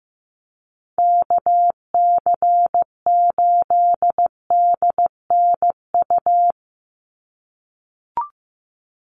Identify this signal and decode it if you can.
Morse code, heard as a keyed tone, about 15 wpm, 700 Hz. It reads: KC8DNU
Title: USA Call Signs 15wpm